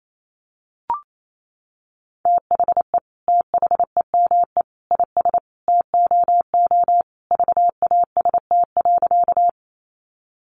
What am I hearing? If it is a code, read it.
Morse code, heard as a keyed tone, about 28 wpm, 700 Hz. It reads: THE THEME IS TOO VAST.
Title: Sentences from Top 1300 Words 28wpm